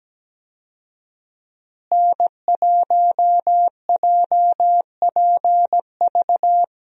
Morse code, heard as a keyed tone, about 17 wpm, 700 Hz. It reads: N1JPV
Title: USA Call Signs - 5 Characters - Repeated 3x - 17wpm